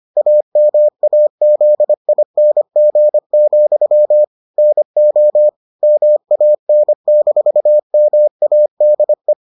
Morse code, heard as a keyed tone, about 25 wpm, 600 Hz. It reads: AMAZING, NO MAN-MADE